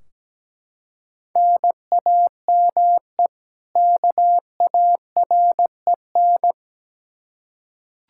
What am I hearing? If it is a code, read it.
Morse code, heard as a keyed tone, about 17 wpm, 700 Hz. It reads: NAME KAREN
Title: Practice QSO - Exchange 1 - Beginning - 17wpm